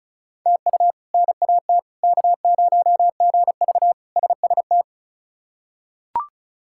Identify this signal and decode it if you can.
Morse code, heard as a keyed tone, about 34 wpm, 700 Hz. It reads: TU NAT K0GV SST
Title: K1USN SST QSO Practice - 35wpm